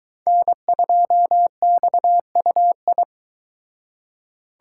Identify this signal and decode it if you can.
Morse code, heard as a keyed tone, about 22 wpm, 700 Hz. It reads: N2XUI